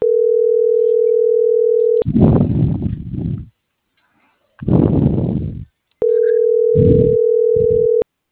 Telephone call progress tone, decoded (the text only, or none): ringback tone